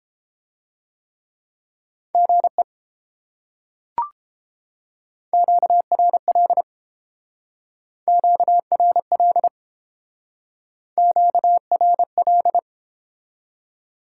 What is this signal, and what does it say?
Morse code, heard as a keyed tone, about 30 wpm, 700 Hz. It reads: GE QRL QRL QRL